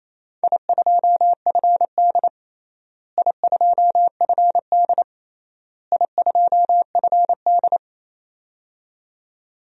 Morse code, heard as a keyed tone, about 28 wpm, 700 Hz. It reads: I2FD I2FD I2FD
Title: DXCC-100 Call Signs - 4 Characters - Repeated 3x - 28wpm